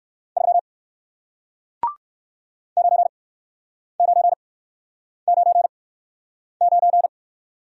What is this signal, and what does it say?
Morse code, heard as a keyed tone, about 59 wpm, spaced as for 9 wpm, 700 Hz. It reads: 3 9999